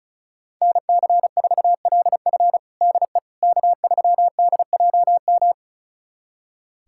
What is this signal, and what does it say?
Morse code, heard as a keyed tone, about 36 wpm, 700 Hz. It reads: NC4LF DE K3DJM
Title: USA Callsign DE Callsign - 5 Characters - 35wpm